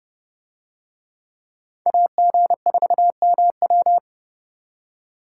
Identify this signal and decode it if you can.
Morse code, heard as a keyed tone, about 27 wpm, 700 Hz. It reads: AG4MW